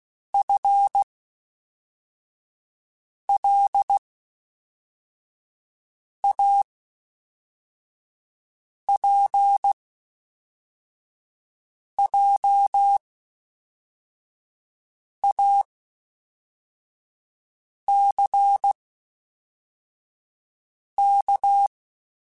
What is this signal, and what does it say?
Morse code, heard as a keyed tone, about 16 wpm, spaced as for 4 wpm, 800 Hz. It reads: FLAPJACK
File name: Spaced / FL Runs sp mp3s / FL Runs sp @5